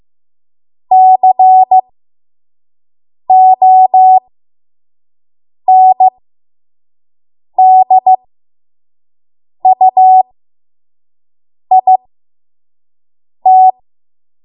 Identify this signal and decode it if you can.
Morse code, heard as a keyed tone, about 15 wpm, spaced as for 5 wpm, 750 Hz. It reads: CONDUIT
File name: Morse / 05WPM